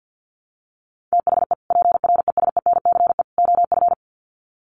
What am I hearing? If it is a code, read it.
Morse code, heard as a keyed tone, about 50 wpm, 700 Hz. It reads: THE PRESENCE OF
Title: Sets of 3 Words 50wpm